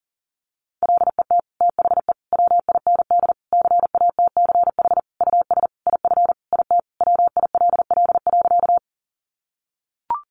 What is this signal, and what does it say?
Morse code, heard as a keyed tone, about 40 wpm, 700 Hz. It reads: LET THE WIND CATCH US IF IT WILL.
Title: Sentences from Top 1500 Words 40wpm